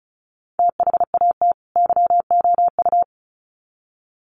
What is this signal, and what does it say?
Morse code, heard as a keyed tone, about 34 wpm, 700 Hz. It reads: THAT YOU
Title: Sets of 2 Words 35wpm